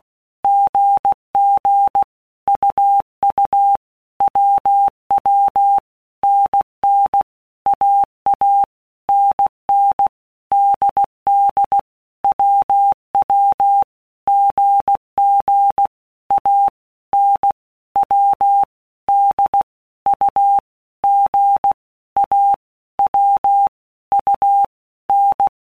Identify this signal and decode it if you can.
Morse code, heard as a keyed tone, about 16 wpm, 800 Hz. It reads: GG UU WW NN AA NN DD WW GG A N W D U G A W U N
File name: All Groups 1-minute Blocks / Groups@15 / AWUNDG